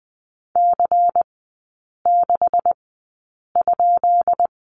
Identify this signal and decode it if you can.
Morse code, heard as a keyed tone, about 20 wpm, spaced as for 8 wpm, 700 Hz. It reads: C6?
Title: 133 Lesson 22e - Question Mark - Three Characters 20wpm